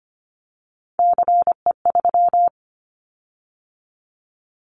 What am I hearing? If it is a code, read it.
Morse code, heard as a keyed tone, about 25 wpm, 700 Hz. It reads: CE3
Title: Callsign Prefixes for Top 100 DXCC - 25wpm